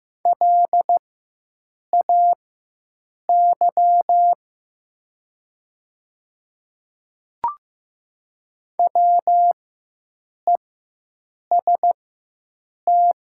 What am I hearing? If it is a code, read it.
Morse code, heard as a keyed tone, about 15 wpm, spaced as for 7 wpm, 700 Hz. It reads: LAY WEST
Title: 4x Character Spacing - Top 500 Words - 15wpm